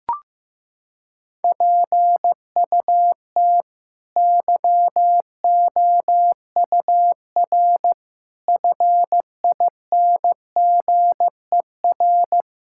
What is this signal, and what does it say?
Morse code, heard as a keyed tone, about 15 wpm, 700 Hz. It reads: PUT YOUR FINGER
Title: Sentences from Top 1600 Words 15wpm